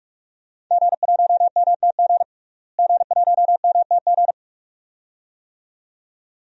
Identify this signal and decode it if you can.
Morse code, heard as a keyed tone, about 46 wpm, 700 Hz. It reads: G1MTG G1MTG
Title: DXCC-100 Call Signs - 5 Characters - Repeated 2x - 45wpm